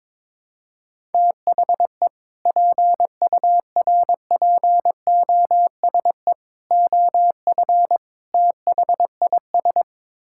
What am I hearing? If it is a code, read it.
Morse code, heard as a keyed tone, about 22 wpm, 700 Hz. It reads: THE PURPOSE OF THIS
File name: practice / 22wpm / Build A Phrase - 5-Word Phrases - 22wpm